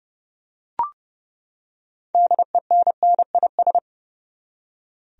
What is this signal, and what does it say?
Morse code, heard as a keyed tone, about 32 wpm, 700 Hz. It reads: DENNIS
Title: USA Common Names - 30wpm